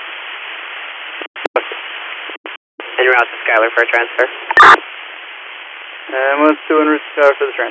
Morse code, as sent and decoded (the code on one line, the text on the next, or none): none